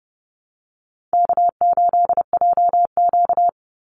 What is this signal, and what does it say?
Morse code, heard as a keyed tone, about 30 wpm, 700 Hz. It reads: K8JQ
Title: Easy Call Signs Repeated 3x - 30wpm